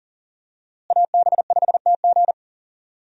Morse code, heard as a keyed tone, about 40 wpm, 700 Hz. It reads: AB5TG
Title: USA Call Signs - 5 Characters - Repeated 3x - 40wpm